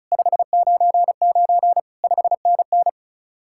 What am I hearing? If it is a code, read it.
Morse code, heard as a keyed tone, about 43 wpm, 700 Hz. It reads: 599 5NN
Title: Speed-Racing Common QSO Phrases Encore - 35wpm